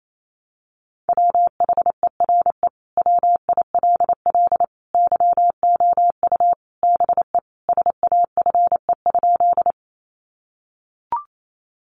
Morse code, heard as a keyed tone, about 28 wpm, 700 Hz. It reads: WHERE WILL YOU BE SAFE?
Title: Sentences from Top 900 Words 28wpm